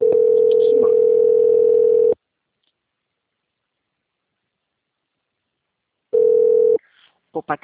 Ringback tone (North America), 440 + 480 Hz.
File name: out- - - - - .88828